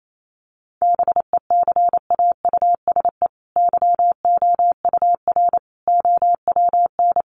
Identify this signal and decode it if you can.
Morse code, heard as a keyed tone, about 28 wpm, 700 Hz. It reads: BECAUSE YOUR OWN
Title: Sentences from Top 1200 Words 28wpm